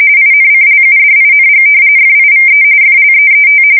RTTY, about 45.45 baud. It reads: INCLUDING FIVE FLEET
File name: fsk170 / battleshipnj